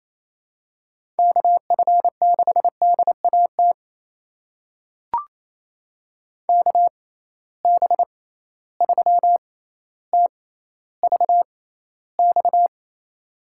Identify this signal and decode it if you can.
Morse code, heard as a keyed tone, about 28 wpm, spaced as for 14 wpm, 700 Hz. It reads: KF6DAT KB3TVX
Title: USA Call Signs - 6 Characters - 6x Character Spacing - 28wpm